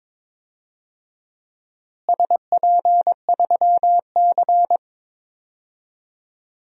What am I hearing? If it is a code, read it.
Morse code, heard as a keyed tone, about 22 wpm, 700 Hz. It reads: SP3C
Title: DXCC-100 Call Signs - 4 Characters - Repeated 3x - 22wpm